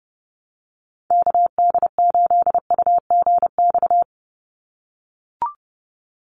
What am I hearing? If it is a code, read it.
Morse code, heard as a keyed tone, about 30 wpm, 700 Hz. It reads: KD8UGX
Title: USA Call Signs 30wpm